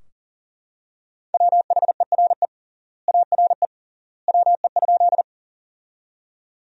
Morse code, heard as a keyed tone, about 37 wpm, 700 Hz. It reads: WHERE ARE WE?
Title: 3x Word-Spacing - Sentences from Top 200 Words - 40wpm